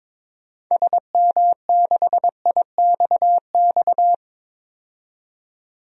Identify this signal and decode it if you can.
Morse code, heard as a keyed tone, about 21 wpm, 700 Hz. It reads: SM6IXX
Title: DXCC-100 Call Signs - 6 Characters - 22wpm